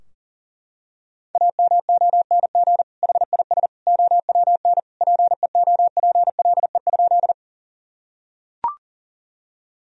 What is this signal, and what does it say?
Morse code, heard as a keyed tone, about 40 wpm, 700 Hz. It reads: AMONG HIS OWN PEOPLE?
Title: Sentences from Top 1000 Words - Review Part 3 - 40wpm